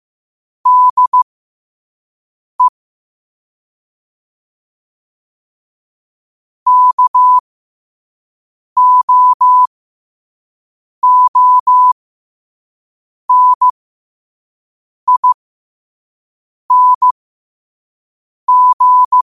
Morse code, heard as a keyed tone, about 15 wpm, spaced as for 5 wpm, 1000 Hz. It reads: DE KOONING
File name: qotdfiles / 5wpm